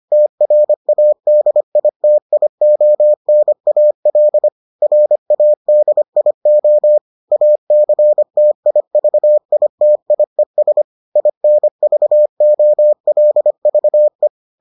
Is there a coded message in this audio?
Yes: Morse code from 0.1 to 14.3 s, heard as a keyed tone, about 25 wpm, 600 Hz. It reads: TRADITIONAL RADIO ACTIVITIES INVOLVE